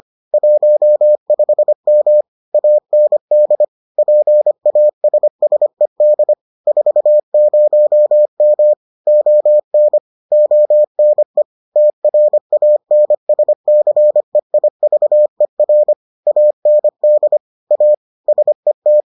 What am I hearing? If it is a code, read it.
Morse code, heard as a keyed tone, about 25 wpm, 600 Hz. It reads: 15M AND PASSED 40M ON ONE TRANSCEIVER AND A SET